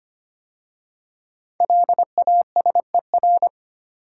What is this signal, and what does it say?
Morse code, heard as a keyed tone, about 25 wpm, 700 Hz. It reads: LASER
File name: 5 Letter Words - First 3 Letters Given - 25wpm